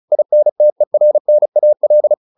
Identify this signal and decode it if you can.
Morse code, heard as a keyed tone, about 34 wpm, 600 Hz. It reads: INTERNAL